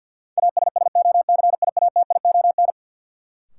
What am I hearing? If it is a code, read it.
Morse code, heard as a keyed tone, about 48 wpm, 700 Hz. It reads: ASSOCIATION
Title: Sentences from Top 1100 Words - New Word Drill Alt - 50wpm